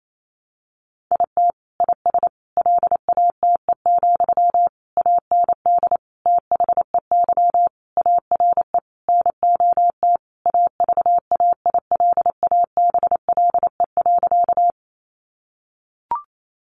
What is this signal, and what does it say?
Morse code, heard as a keyed tone, about 28 wpm, 700 Hz. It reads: IT IS LATE, AND THEY ARE NOT AVAILABLE.
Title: Sentences from Top 600 Words 28wpm